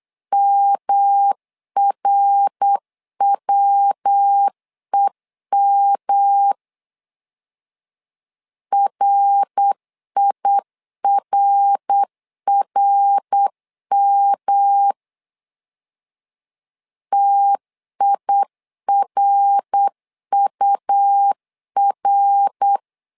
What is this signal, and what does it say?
Morse code, heard as a keyed tone, about 8 wpm, 800 Hz. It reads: MRWEM RIRRM TIRUR
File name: LEZIONI / 3 MWRO / 3 MWR 8wpm 2444